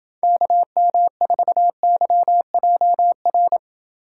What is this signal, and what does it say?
Morse code, heard as a keyed tone, about 28 wpm, 700 Hz. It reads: KM4YJR